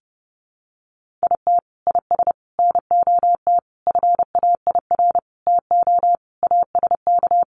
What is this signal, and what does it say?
Morse code, heard as a keyed tone, about 30 wpm, 700 Hz. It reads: IT IS NOT FAIR TO ASK
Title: Sentences from Top 1300 Words 30wpm